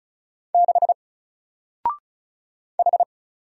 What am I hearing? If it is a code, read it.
Morse code, heard as a keyed tone, about 33 wpm, 700 Hz. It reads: 6 H
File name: Single-Letter-Number-35wpm